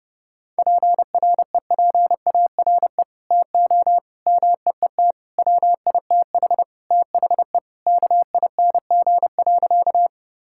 Morse code, heard as a keyed tone, about 30 wpm, 700 Hz. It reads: PREPARE TO MEET WITH THE KING.
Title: Sentences from Top 1000 Words - Review Part 6 - 30wpm